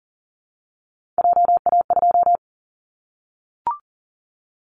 Morse code, heard as a keyed tone, about 40 wpm, 700 Hz. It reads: JA2
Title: Callsign Prefixes for Top 25 DXCC - 40wpm